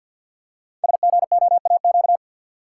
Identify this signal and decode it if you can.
Morse code, heard as a keyed tone, about 51 wpm, 700 Hz. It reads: SGOAQ
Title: Code Groups - 5 Letters - 50wpm